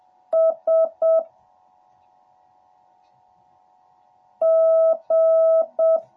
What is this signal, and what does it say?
Morse code, heard as a keyed tone, about 7 wpm, 650 Hz. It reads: S G